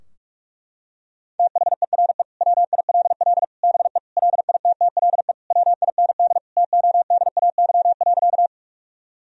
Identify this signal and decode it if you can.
Morse code, heard as a keyed tone, about 41 wpm, 700 Hz. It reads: THERE WILL BE LITTLE WIND TODAY.
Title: Sentences from Top 1000 Words - Review Part 2 - 45wpm